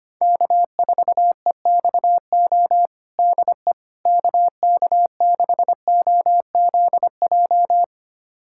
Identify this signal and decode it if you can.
Morse code, heard as a keyed tone, about 25 wpm, 700 Hz. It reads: K4EXO DE KK6OZJ
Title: USA Callsign DE Callsign 25wpm